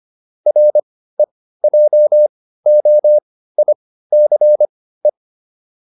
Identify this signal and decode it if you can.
Morse code, heard as a keyed tone, about 25 wpm, spaced as for 15 wpm, 600 Hz. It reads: REJOICE